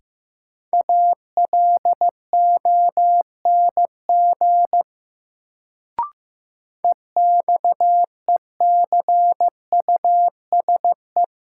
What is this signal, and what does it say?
Morse code, heard as a keyed tone, about 15 wpm, 700 Hz. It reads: ALONG EXECUSE
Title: Sentences from Top 700 Words - New Word Drill - 15wpm